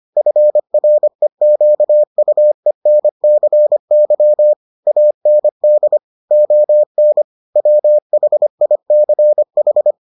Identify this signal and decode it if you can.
Morse code, heard as a keyed tone, about 25 wpm, 600 Hz. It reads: FREQUENCY AND ON WHICH